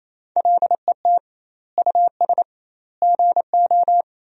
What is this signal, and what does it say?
Morse code, heard as a keyed tone, about 28 wpm, 700 Hz. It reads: LET US GO
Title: 2x Word-Spacing - Sentences from Top 300 Words - 28wpm